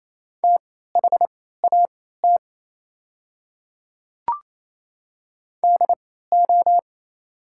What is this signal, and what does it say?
Morse code, heard as a keyed tone, about 28 wpm, spaced as for 16 wpm, 700 Hz. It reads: THAT DO
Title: 3x Character Spacing - Top 100 Words - 28wpm